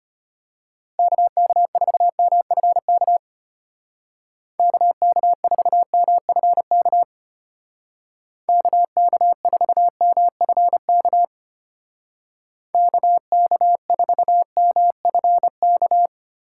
Morse code, heard as a keyed tone, about 33 wpm, 700 Hz. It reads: KK4MFK KK4MFK KK4MFK KK4MFK